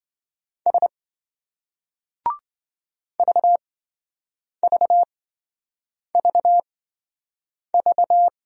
Morse code, heard as a keyed tone, about 27 wpm, spaced as for 7 wpm, 700 Hz. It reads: S VVVV